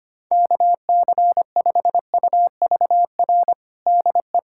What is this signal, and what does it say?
Morse code, heard as a keyed tone, about 25 wpm, 700 Hz. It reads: KC5UVR DE
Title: USA Callsign DE Callsign 25wpm